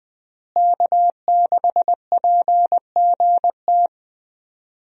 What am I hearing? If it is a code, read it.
Morse code, heard as a keyed tone, about 21 wpm, 700 Hz. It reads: K6PGT
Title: USA Call Signs - 5 Characters - Repeated 3x - 20wpm